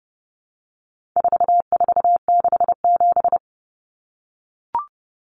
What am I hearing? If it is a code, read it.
Morse code, heard as a keyed tone, about 30 wpm, 700 Hz. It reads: 4467
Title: Numbers - 4-Digit 30wpm